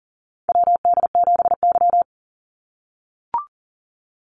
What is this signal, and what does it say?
Morse code, heard as a keyed tone, about 40 wpm, 700 Hz. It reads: WD7Y
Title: NA SOTA Chaser Callsigns 40wpm